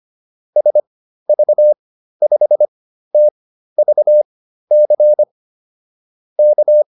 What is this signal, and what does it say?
Morse code, heard as a keyed tone, about 25 wpm, spaced as for 13 wpm, 600 Hz. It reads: SV5TVC K
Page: intermediate-practice-files